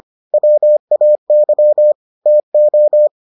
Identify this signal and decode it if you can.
Morse code, heard as a keyed tone, about 25 wpm, 600 Hz. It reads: WAY TO